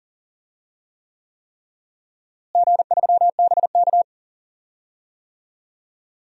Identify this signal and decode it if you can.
Morse code, heard as a keyed tone, about 40 wpm, 700 Hz. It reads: G3BK
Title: DXCC-100 Call Signs - 4 Characters - Repeated 3x - 40wpm